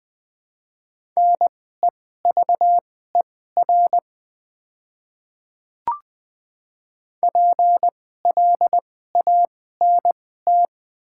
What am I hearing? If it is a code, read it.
Morse code, heard as a keyed tone, about 20 wpm, spaced as for 14 wpm, 700 Hz. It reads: NEVER PLANT
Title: 2x Character Spacing - Top 300 Words - 20wpm